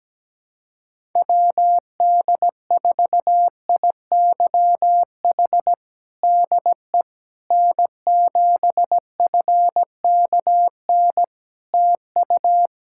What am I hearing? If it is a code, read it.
Morse code, heard as a keyed tone, about 17 wpm, 700 Hz. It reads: WD4IYH DE N7FKN TU
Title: Practice QSO - Exchange 4 - Ending - 17wpm